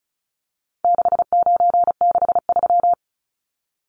Morse code, heard as a keyed tone, about 34 wpm, 700 Hz. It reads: 6963
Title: Numbers - 4-Digit 35wpm